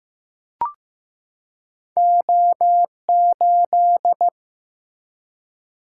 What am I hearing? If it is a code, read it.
Morse code, heard as a keyed tone, about 15 wpm, 700 Hz. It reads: O8
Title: Single-Letter Single-Number 15wpm